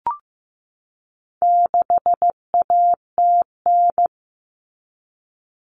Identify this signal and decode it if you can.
Morse code, heard as a keyed tone, about 15 wpm, 700 Hz. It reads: 6ATN
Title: All Cut Numbers - 4-Digit 15wpm